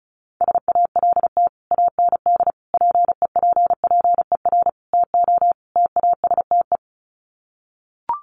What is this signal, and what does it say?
Morse code, heard as a keyed tone, about 34 wpm, 700 Hz. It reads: SALT AND PEPPER TO TASTE
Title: Sets of 5 Words 35wpm